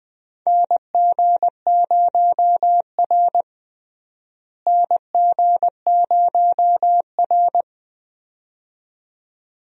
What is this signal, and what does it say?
Morse code, heard as a keyed tone, about 20 wpm, 700 Hz. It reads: NG0R NG0R
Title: USA Call Signs - 4 Characters - Repeated 2x - 20wpm